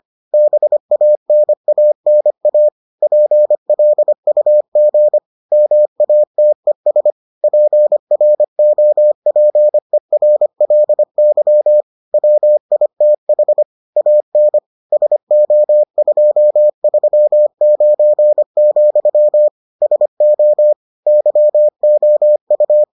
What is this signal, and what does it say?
Morse code, heard as a keyed tone, about 25 wpm, 600 Hz. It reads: BANANA PLUG MATES PROPERLY WITH AN SO239, SO YOU